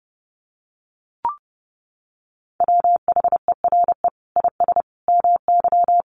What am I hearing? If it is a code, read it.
Morse code, heard as a keyed tone, about 30 wpm, 700 Hz. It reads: WHERE IS MY
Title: Sentences from Top 1400 Words 30wpm